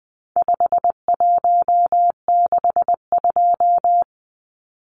Morse code, heard as a keyed tone, about 20 wpm, 700 Hz. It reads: 5162
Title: Numbers - 4-Digit 20wpm